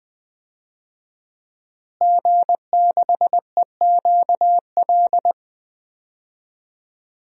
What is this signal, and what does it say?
Morse code, heard as a keyed tone, about 20 wpm, 700 Hz. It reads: G6EQL
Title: DXCC-100 Call Signs - 5 Characters - Repeated 3x - 20wpm